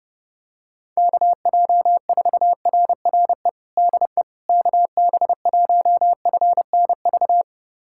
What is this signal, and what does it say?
Morse code, heard as a keyed tone, about 31 wpm, 700 Hz. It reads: KJ4RRE DE KB1FNV
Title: USA Callsign DE Callsign - 6 Characters - 30wpm